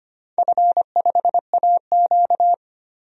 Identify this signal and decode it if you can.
Morse code, heard as a keyed tone, about 25 wpm, 700 Hz. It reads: F5AQ
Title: DXCC-100 Call Signs - 4 Characters - 25wpm